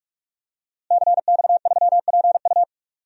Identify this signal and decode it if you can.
Morse code, heard as a keyed tone, about 47 wpm, 700 Hz. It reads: CX3PU
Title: DXCC-100 Call Signs - 5 Characters - Repeated 2x - 45wpm